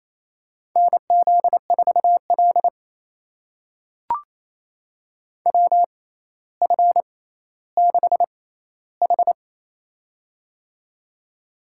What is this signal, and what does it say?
Morse code, heard as a keyed tone, about 28 wpm, spaced as for 14 wpm, 700 Hz. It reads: NZ4L WF6H
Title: USA Call Signs - 6x Character Spacing - 28wpm